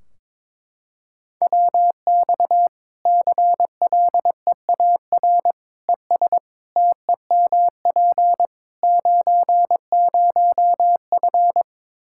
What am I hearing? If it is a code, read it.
Morse code, heard as a keyed tone, about 22 wpm, 700 Hz. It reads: WX CLEAR ES TEMP 90F
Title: Practice QSO - Exchange 2 - Info - 22wpm